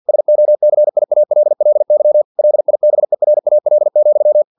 Morse code, heard as a keyed tone, about 50 wpm, 600 Hz. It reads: SOCIALLY LIBERAL,
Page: Never 49wpm